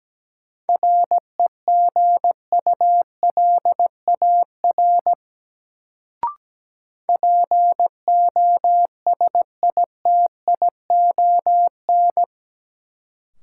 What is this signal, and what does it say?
Morse code, heard as a keyed tone, about 17 wpm, 700 Hz. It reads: REGULAR POSITION
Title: Sentences from Top 900 Words - New Word Drill - 17wpm